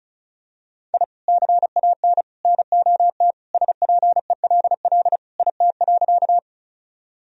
Morse code, heard as a keyed tone, about 36 wpm, 700 Hz. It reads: I CAN NOT SPELL IT.
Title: Sentences from Top 1000 Words - Review Part 1 - 35wpm